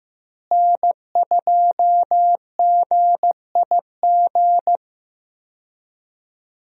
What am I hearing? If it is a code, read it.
Morse code, heard as a keyed tone, about 15 wpm, 700 Hz. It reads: N2GIG
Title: USA Call Signs - 5 Characters - Repeated 3x - 15wpm